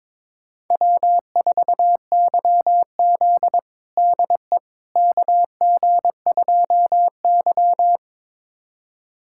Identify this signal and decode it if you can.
Morse code, heard as a keyed tone, about 22 wpm, 700 Hz. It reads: W4YZ DE KG2Y
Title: USA Callsign DE Callsign - 4 Characters - 22wpm